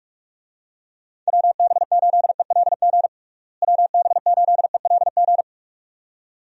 Morse code, heard as a keyed tone, about 42 wpm, 700 Hz. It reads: WB8ELG WB8ELG
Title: USA Call Signs - 6 Characters - Repeated 2x - 45wpm